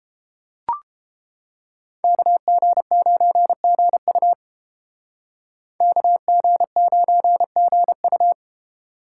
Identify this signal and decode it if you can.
Morse code, heard as a keyed tone, about 30 wpm, 700 Hz. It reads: KG9GU KG9GU